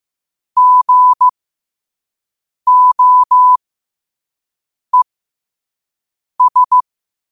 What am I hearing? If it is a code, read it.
Morse code, heard as a keyed tone, about 15 wpm, spaced as for 5 wpm, 1000 Hz. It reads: GOES
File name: qotdfiles / 5wpm